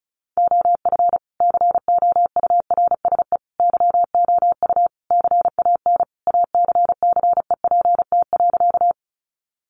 Morse code, heard as a keyed tone, about 34 wpm, 700 Hz. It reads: OF COURSE YOU CAN ACCEPT.
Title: Sentences from Top 1300 Words 35wpm